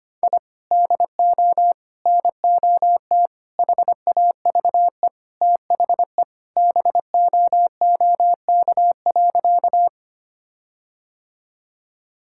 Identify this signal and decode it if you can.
Morse code, heard as a keyed tone, about 25 wpm, 700 Hz. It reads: I DO NOT HAVE THE BOOK.
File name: practice / 25wpm / Sentences with 4-Word Phrases - 25wpm